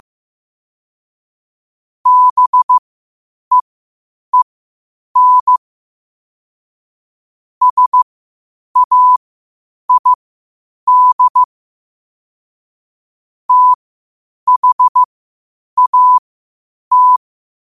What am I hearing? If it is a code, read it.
Morse code, heard as a keyed tone, about 15 wpm, spaced as for 8 wpm, 1000 Hz. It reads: BEEN SAID THAT
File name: qotdfiles / 8wpm